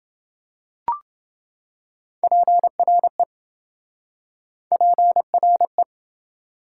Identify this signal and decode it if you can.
Morse code, heard as a keyed tone, about 27 wpm, 700 Hz. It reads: PRE PRE